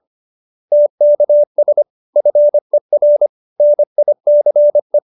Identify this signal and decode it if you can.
Morse code, heard as a keyed tone, about 25 wpm, 590 Hz. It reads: TKS FER NICE
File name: QSO315 25